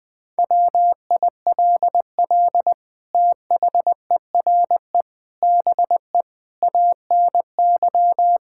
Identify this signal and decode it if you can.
Morse code, heard as a keyed tone, about 20 wpm, 700 Hz. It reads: WILL THERE BE ANY
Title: Sentences from Top 1600 Words 20wpm